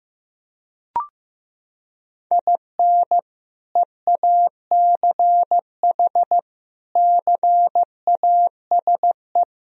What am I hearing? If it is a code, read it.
Morse code, heard as a keyed tone, about 15 wpm, 700 Hz. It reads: IN EACH CASE
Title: Sentences from Top 1000 Words - Review Part 1 - 15wpm